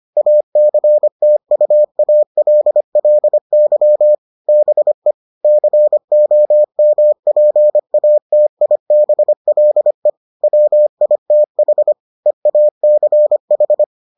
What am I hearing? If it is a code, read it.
Morse code, heard as a keyed tone, about 25 wpm, 600 Hz. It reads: ACTUALLY BE COMPATIBLE WITH EACH